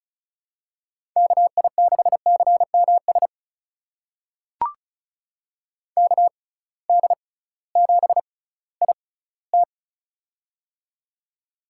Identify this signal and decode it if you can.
Morse code, heard as a keyed tone, about 36 wpm, spaced as for 19 wpm, 700 Hz. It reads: KI6CMS KD7IT
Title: USA Call Signs - 6x Character Spacing - 35wpm